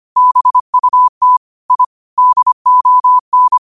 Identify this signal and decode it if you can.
Morse code, heard as a keyed tone, about 25 wpm, 1000 Hz. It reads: BUT I DON
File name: qotdfiles / 25wpm